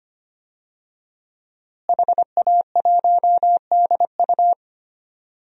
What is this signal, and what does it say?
Morse code, heard as a keyed tone, about 25 wpm, 700 Hz. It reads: HA1DU
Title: DXCC-100 Call Signs - 5 Characters - Repeated 2x - 25wpm